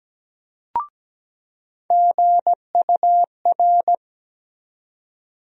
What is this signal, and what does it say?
Morse code, heard as a keyed tone, about 17 wpm, 700 Hz. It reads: GUR
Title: USA Call Sign Postfixes - 3 Letters 17wpm